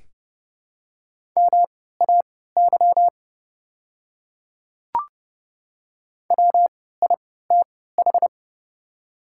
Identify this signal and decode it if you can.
Morse code, heard as a keyed tone, about 30 wpm, spaced as for 17 wpm, 700 Hz. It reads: MAY WITH
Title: 3x Character Spacing - Top 500 Words - 30wpm